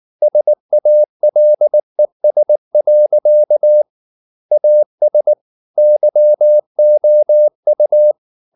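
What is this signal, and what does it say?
Morse code, heard as a keyed tone, about 19 wpm, 600 Hz. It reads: SALES. AS YOU
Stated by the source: Morse code at 19 wpm from CW med Gnister, reading Ken Follett's novel Never in English.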